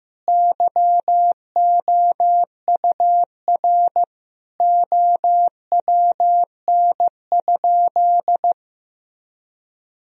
Morse code, heard as a keyed tone, about 15 wpm, 700 Hz. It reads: YOUR OWN?
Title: Sentences from Top 1000 Words - Review Part 4 - 15wpm